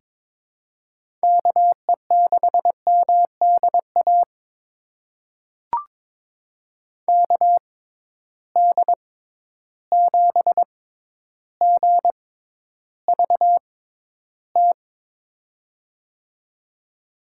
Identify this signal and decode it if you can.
Morse code, heard as a keyed tone, about 23 wpm, spaced as for 11 wpm, 700 Hz. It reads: KE6MDA KD7GVT
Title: USA Call Signs - 6 Characters - 6x Character Spacing - 22wpm